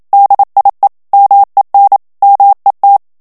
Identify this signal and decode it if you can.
Morse code, heard as a keyed tone, about 27 wpm, 800 Hz. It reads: DIE MEN MET
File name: HOLLANDAIS texte clair 1 vitesse 26 mots minute